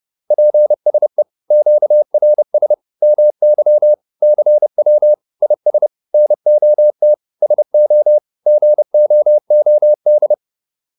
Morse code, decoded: PSE QRS MY CW IS NOT SO GOOD

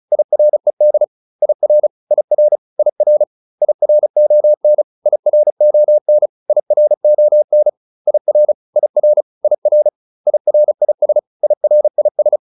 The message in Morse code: IRED IR IR IR IRON IRON IRON IR IR IR IRIS IRIS

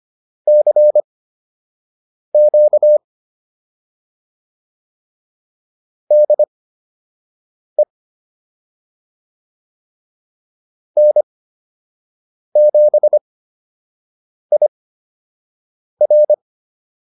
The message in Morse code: CQ DE N7IR

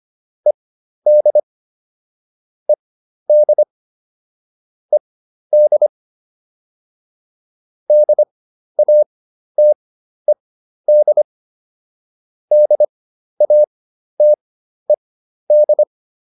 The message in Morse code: ED ED ED DATED DATED

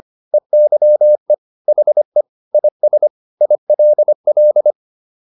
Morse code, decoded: EYE HE IS ILL

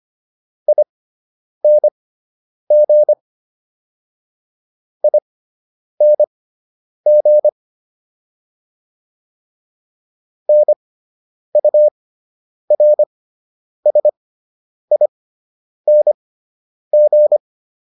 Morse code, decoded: ING ING NURSING

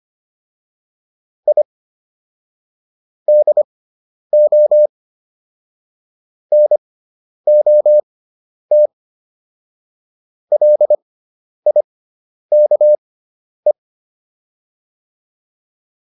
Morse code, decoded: I DO NOT LIKE

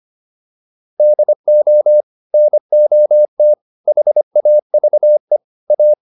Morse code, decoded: DO NOT HAVE A

